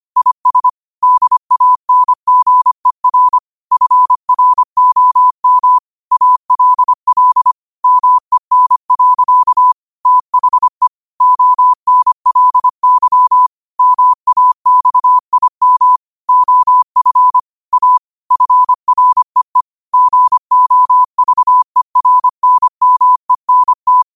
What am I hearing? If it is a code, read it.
Morse code, heard as a keyed tone, about 25 wpm, 1000 Hz. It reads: IS DANGER FROM ALL MEN. THE ONLY MAXIM OF A FREE GOVERNMENT
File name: qotdfiles / 25wpm